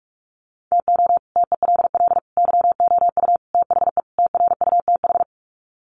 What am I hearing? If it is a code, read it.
Morse code, heard as a keyed tone, about 44 wpm, 700 Hz. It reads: TO TELL YOU THE TRUTH
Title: Sets of 5 Words 45wpm